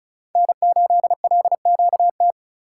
Morse code, heard as a keyed tone, about 36 wpm, 700 Hz. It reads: N8LQT
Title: USA Call Signs - 5 Characters - 4x Character Spacing - 35wpm